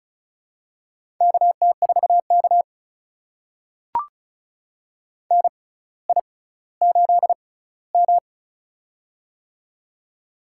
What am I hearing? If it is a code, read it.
Morse code, heard as a keyed tone, about 36 wpm, spaced as for 18 wpm, 700 Hz. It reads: KT4K NI8M
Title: USA Call Signs - 4 Characters - 6x Character Spacing - 35wpm